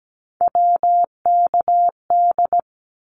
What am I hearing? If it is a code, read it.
Morse code, heard as a keyed tone, about 17 wpm, 700 Hz. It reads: WKD
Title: Abbreviations 17wpm